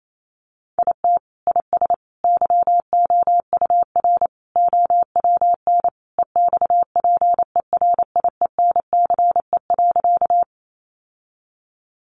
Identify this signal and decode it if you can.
Morse code, heard as a keyed tone, about 28 wpm, 700 Hz. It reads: IT IS YOUR OWN EXPERIENCE.
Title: Sentences from Top 600 Words 28wpm